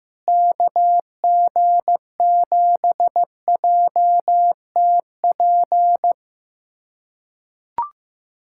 Morse code, heard as a keyed tone, about 15 wpm, 700 Hz. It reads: KG7JTP
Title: USA Callsign DE Callsign - 6 Characters - 15wpm